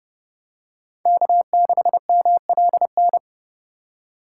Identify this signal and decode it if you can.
Morse code, heard as a keyed tone, about 32 wpm, 700 Hz. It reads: K6MLN
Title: USA Call Signs - 5 Characters - 3x Character Spacing - 30wpm